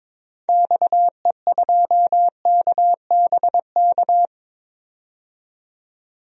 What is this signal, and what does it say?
Morse code, heard as a keyed tone, about 21 wpm, 700 Hz. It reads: XE2KBK
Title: DXCC-100 Call Signs - 6 Characters - 22wpm